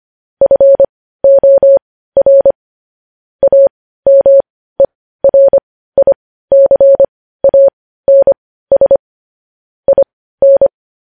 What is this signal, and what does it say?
Morse code, heard as a keyed tone, about 25 wpm, spaced as for 15 wpm, 550 Hz. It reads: FOR AMERICANS IN
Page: news-headlines-in-morse-code-at-15-wpm